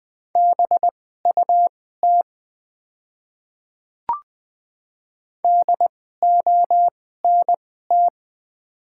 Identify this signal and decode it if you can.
Morse code, heard as a keyed tone, about 20 wpm, spaced as for 15 wpm, 700 Hz. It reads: BUT DONT
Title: 2x Character Spacing - Top 500 Words - 20wpm